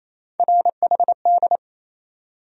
Morse code, heard as a keyed tone, about 28 wpm, 700 Hz. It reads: RHD
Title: USA Call Sign Postfixes - 3 Letters 28wpm